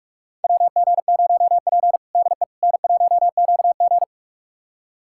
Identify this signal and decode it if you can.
Morse code, heard as a keyed tone, about 46 wpm, 700 Hz. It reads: WG0P DE N1QG
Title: USA Callsign DE Callsign - 4 Characters - 45wpm